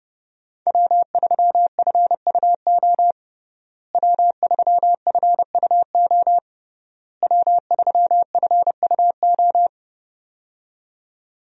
Morse code, heard as a keyed tone, about 30 wpm, 700 Hz. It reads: W3FUO W3FUO W3FUO
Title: USA Call Signs - 5 Characters - Repeated 3x - 30wpm